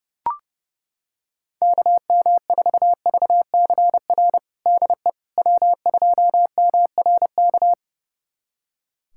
Morse code, heard as a keyed tone, about 30 wpm, 700 Hz. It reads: KM4VCR DE W2MRK
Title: USA Callsign DE Callsign 30wpm